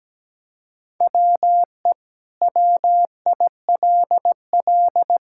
Morse code, heard as a keyed tone, about 17 wpm, 700 Hz. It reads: WE WILL
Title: Sentences from Top 1000 Words - Review Part 5 - 17wpm